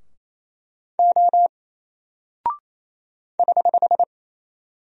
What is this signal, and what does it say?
Morse code, heard as a keyed tone, about 27 wpm, 700 Hz. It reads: O <HH>
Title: Single Character (Letter:Number:Punctuation:Prosign) Alt 28wpm